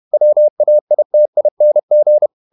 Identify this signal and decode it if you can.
Morse code, heard as a keyed tone, about 31 wpm, 600 Hz. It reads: WAITING
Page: Never 31wpm | CW med Gnister